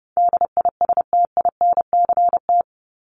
Morse code, heard as a keyed tone, about 30 wpm, 700 Hz. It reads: DISTINCT
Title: Top 1501-1750 Words 30wpm